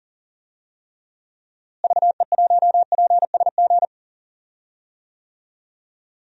Morse code, heard as a keyed tone, about 38 wpm, 700 Hz. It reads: VE1PSG
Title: DXCC-100 Call Signs - 6 Characters - Repeated 3x - 40wpm